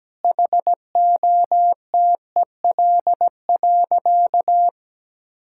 Morse code, heard as a keyed tone, about 17 wpm, 700 Hz. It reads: HOTEL.
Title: Sentences from Top 1600 Words 17wpm